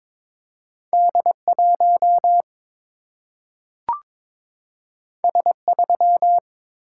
Morse code, heard as a keyed tone, about 22 wpm, 700 Hz. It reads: D1 S3
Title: Single-Letter Single-Number 22wpm